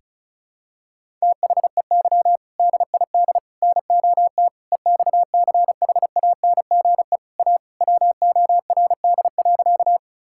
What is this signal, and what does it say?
Morse code, heard as a keyed tone, about 36 wpm, 700 Hz. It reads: THEY DID NOT EXCHANGE A WORD.
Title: Sentences from Top 1000 Words - Review Part 6 - 35wpm